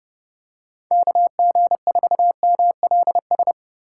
Morse code, heard as a keyed tone, about 29 wpm, 700 Hz. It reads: KG4MLS